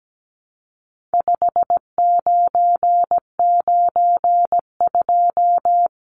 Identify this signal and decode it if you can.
Morse code, heard as a keyed tone, about 17 wpm, 700 Hz. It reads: 5992
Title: Numbers - 4-Digit 17wpm